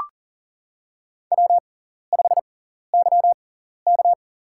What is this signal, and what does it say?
Morse code, heard as a keyed tone, about 40 wpm, spaced as for 14 wpm, 700 Hz. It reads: W5YK
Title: USA Call Signs - 4 Characters - 6x Character Spacing - 40wpm